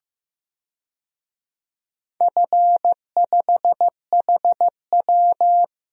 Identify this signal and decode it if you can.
Morse code, heard as a keyed tone, about 15 wpm, 700 Hz. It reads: F5HW
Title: DXCC-100 Call Signs - 4 Characters - Repeated 3x - 15wpm